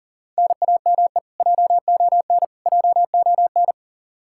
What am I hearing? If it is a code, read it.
Morse code, heard as a keyed tone, about 40 wpm, 700 Hz. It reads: NAME JON JON
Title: K1USN SST QSO Practice 40wpm